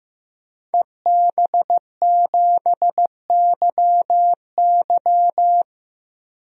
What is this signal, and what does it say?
Morse code, heard as a keyed tone, about 15 wpm, 700 Hz. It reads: EB7YY
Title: DXCC-100 Call Signs - 5 Characters - Repeated 3x - 15wpm